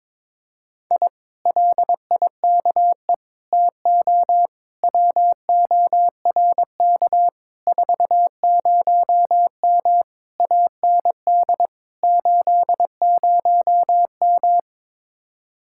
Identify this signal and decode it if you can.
Morse code, heard as a keyed tone, about 22 wpm, 700 Hz. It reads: I LIKE TO WORK 40M AND 80M
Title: Speed-Racing Common QSO Phrases - 22wpm